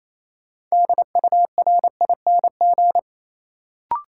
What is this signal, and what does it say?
Morse code, heard as a keyed tone, about 28 wpm, 700 Hz. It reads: DURING
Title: Sets of 2 Words - New Word Drill - 28wpm